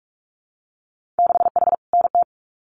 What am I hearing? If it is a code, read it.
Morse code, heard as a keyed tone, about 44 wpm, 700 Hz. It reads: 6H NT
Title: Winter Field Day Exchanges Encore 45wpm